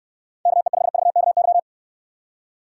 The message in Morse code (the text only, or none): KE5VKY